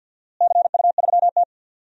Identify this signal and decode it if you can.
Morse code, heard as a keyed tone, about 53 wpm, 700 Hz. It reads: CU3T
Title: DXCC-100 Call Signs - 4 Characters - Repeated 3x - 50wpm